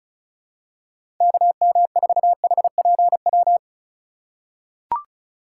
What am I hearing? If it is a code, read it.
Morse code, heard as a keyed tone, about 36 wpm, 700 Hz. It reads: KM4HPW
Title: USA Call Signs - 6 Characters - 4x Character Spacing - 35wpm